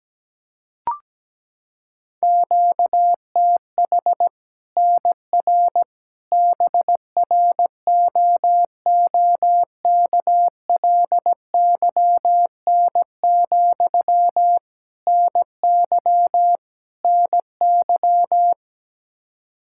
Morse code, heard as a keyed tone, about 17 wpm, 700 Hz. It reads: QTH NR BROOKLYN, NY NY
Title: Practice QSO - Exchange 1 - Beginning - 17wpm